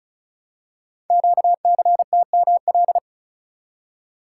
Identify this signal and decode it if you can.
Morse code, heard as a keyed tone, about 36 wpm, 690 Hz. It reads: QCTML
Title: Code Groups - 5 Letters - 35wpm